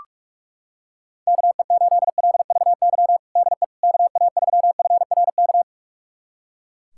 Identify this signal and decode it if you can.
Morse code, heard as a keyed tone, about 41 wpm, 690 Hz. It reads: KE8LVY DE KA3FRK
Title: USA Callsign DE Callsign - 6 Characters - 45wpm